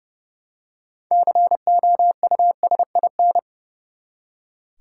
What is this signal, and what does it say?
Morse code, heard as a keyed tone, about 32 wpm, 700 Hz. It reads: COUSIN
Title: Six Letter Words - 30wpm